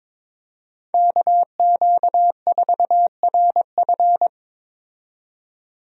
Morse code, heard as a keyed tone, about 22 wpm, 700 Hz. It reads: KQ4RF
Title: USA Call Signs - 5 Characters - 22wpm